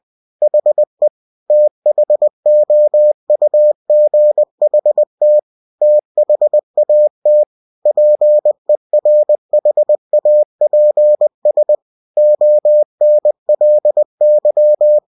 Morse code, heard as a keyed tone, about 20 wpm, 600 Hz. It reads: HE THOUGHT THAT PERHAPS ONLY